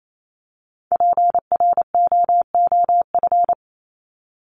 Morse code, heard as a keyed tone, about 28 wpm, 700 Hz. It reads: PROOF
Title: Top 1501-1750 Words 28wpm